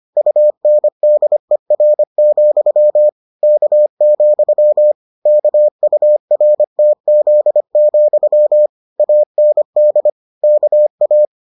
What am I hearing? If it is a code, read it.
Morse code, heard as a keyed tone, about 25 wpm, 600 Hz. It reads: UNDER, K, KURTZ, AND KA